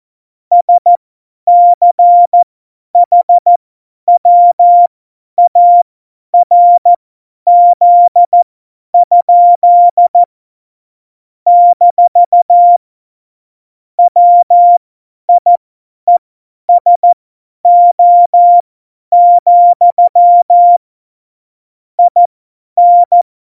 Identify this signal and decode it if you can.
Morse code, heard as a keyed tone, about 14 wpm, spaced as for 10 wpm, 700 Hz. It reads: SCHWARZ? - WIESO, IN
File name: Morsekurs / Witze / Witz 10. September 2019, 50 F70